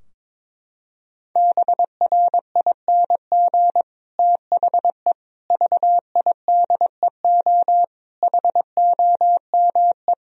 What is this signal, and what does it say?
Morse code, heard as a keyed tone, about 22 wpm, 700 Hz. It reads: BRING THE VIDEO HOME
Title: Sentences from Top 1000 Words - Review Part 4 - 22wpm